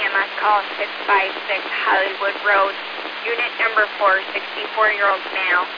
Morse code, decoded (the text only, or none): none